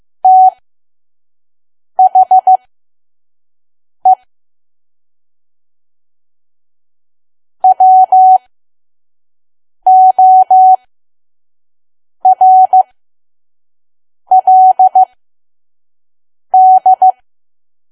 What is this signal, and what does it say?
Morse code, heard as a keyed tone, about 15 wpm, spaced as for 5 wpm, 750 Hz. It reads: THE WORLD
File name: Archive / 5 WPM files / PM